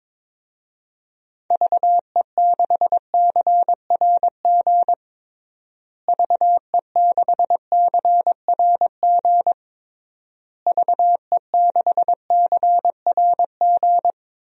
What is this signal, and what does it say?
Morse code, heard as a keyed tone, about 22 wpm, 700 Hz. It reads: VE6CRG VE6CRG VE6CRG
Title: DXCC-100 Call Signs - 6 Characters - Repeated 3x - 22wpm